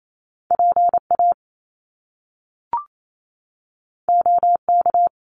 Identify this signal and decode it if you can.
Morse code, heard as a keyed tone, about 28 wpm, 700 Hz. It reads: PA OK
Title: Top Two Letter Combinations in Words 28wpm